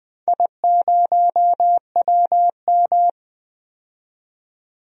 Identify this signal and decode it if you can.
Morse code, heard as a keyed tone, about 20 wpm, 700 Hz. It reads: I0WM
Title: DXCC-100 Call Signs - 4 Characters - 20wpm